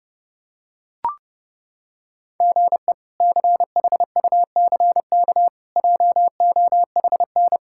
Morse code, heard as a keyed tone, about 30 wpm, 700 Hz. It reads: GE CHUCK JOHN
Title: K1USN SST QSO Practice - 30wpm